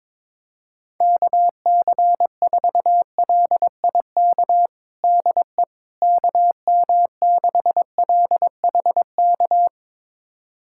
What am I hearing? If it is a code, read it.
Morse code, heard as a keyed tone, about 21 wpm, 700 Hz. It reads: KC4LIK DE KM6LHK
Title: USA Callsign DE Callsign - 6 Characters - 22wpm